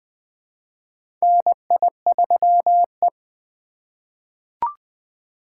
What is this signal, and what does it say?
Morse code, heard as a keyed tone, about 20 wpm, 700 Hz. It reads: NI3E
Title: USA Call Signs - 4 Characters - 2x Character Spacing - 20wpm